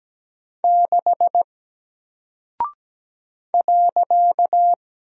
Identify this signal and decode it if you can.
Morse code, heard as a keyed tone, about 17 wpm, 700 Hz. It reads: 6 .
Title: Single Character (Letter:Number:Punctuation:Prosign) Alt 17wpm